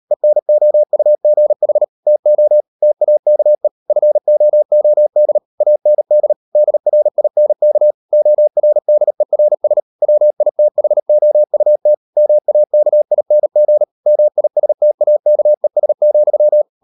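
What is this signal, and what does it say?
Morse code, heard as a keyed tone, about 38 wpm, 600 Hz. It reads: ENOUGH TO TAKE FOOD AND DRINK ORDERS WITHOUT MAKING MISTAKES,
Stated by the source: Morse code practice at 38 WPM, sending Ken Follett's novel Never in English, from CW med Gnister.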